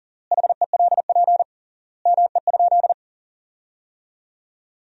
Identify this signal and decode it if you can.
Morse code, heard as a keyed tone, about 40 wpm, 700 Hz. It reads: HELP ME?
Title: 3x Word-Spacing - Sentences from Top 200 Words - 40wpm